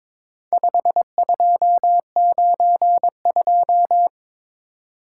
Morse code, heard as a keyed tone, about 23 wpm, 700 Hz. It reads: 5292
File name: practice / 22wpm / Four Letter-Number Combinations - 22wpm